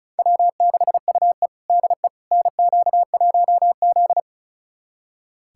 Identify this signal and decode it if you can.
Morse code, heard as a keyed tone, about 34 wpm, 700 Hz. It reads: W6UE DE NQ1Z
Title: USA Callsign DE Callsign - 4 Characters - 35wpm